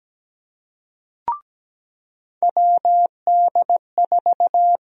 Morse code, heard as a keyed tone, about 17 wpm, 700 Hz. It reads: WD4
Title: USA Call Sign Prefixes - 17wpm